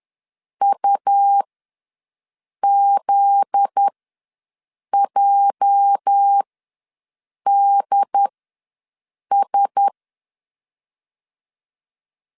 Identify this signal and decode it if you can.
Morse code, heard as a keyed tone, about 10 wpm, spaced as for 6 wpm, 800 Hz. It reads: UZJDS
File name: LEZIONI / 9 28 ws 10wpm 4124